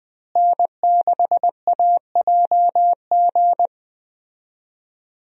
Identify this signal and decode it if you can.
Morse code, heard as a keyed tone, about 20 wpm, 700 Hz. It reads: N6AJG
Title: USA Call Signs - 5 Characters - Repeated 2x - 20wpm